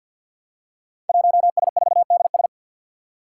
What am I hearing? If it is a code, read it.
Morse code, heard as a keyed tone, about 47 wpm, 700 Hz. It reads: 1S4DS
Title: Five Challenging Letter-Number Combinations 50wpm